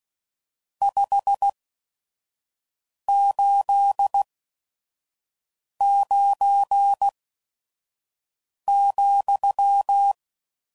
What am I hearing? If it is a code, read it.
Morse code, heard as a keyed tone, about 16 wpm, spaced as for 5 wpm, 800 Hz. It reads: 589,
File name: CW Code Groups @5WPM / RST's / Various RST No's